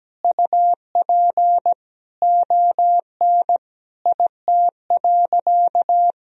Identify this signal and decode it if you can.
Morse code, heard as a keyed tone, about 17 wpm, 700 Hz. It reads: UP ON IT.
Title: Sentences from Top 1000 Words - Review Part 3 - 17wpm